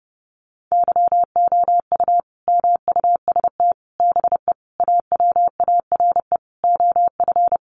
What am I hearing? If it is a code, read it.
Morse code, heard as a keyed tone, about 30 wpm, 700 Hz. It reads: YOU MUST BE AWARE OF
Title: Sentences from Top 1200 Words 30wpm